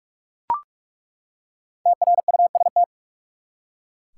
Morse code, heard as a keyed tone, about 41 wpm, 700 Hz. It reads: TRUST
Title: Five Letter Words - 45wpm